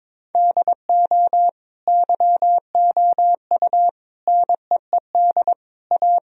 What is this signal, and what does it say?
Morse code, heard as a keyed tone, about 22 wpm, 700 Hz. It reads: DO YOU NEED A
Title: Sentences from Top 1000 Words - Review Part 4 - 22wpm